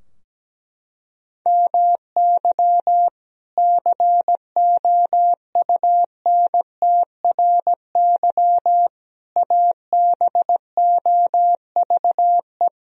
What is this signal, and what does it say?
Morse code, heard as a keyed tone, about 17 wpm, 700 Hz. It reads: MY COUNTRY ABOVE
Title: Sentences from Top 1000 Words - Review Part 2 - 17wpm